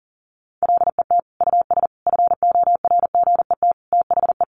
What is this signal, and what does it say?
Morse code, heard as a keyed tone, about 40 wpm, 700 Hz. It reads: LET US FORGET THE
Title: Sentences from Top 1200 Words 40wpm